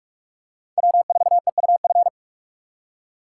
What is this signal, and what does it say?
Morse code, heard as a keyed tone, about 44 wpm, 700 Hz. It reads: W4EUF